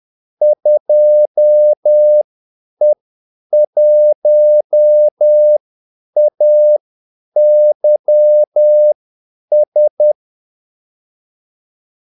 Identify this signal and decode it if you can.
Morse code, heard as a keyed tone, about 10 wpm, spaced as for 8 wpm, 600 Hz. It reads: 2E1AYS
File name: Callsigns-10-8wpm